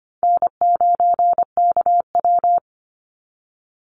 Morse code, heard as a keyed tone, about 25 wpm, 700 Hz. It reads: N9KW
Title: NA SOTA Chaser Callsigns 25wpm